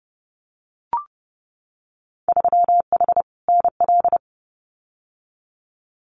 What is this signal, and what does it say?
Morse code, heard as a keyed tone, about 30 wpm, 700 Hz. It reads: 3H NL
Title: Winter Field Day Exchanges Encore 30wpm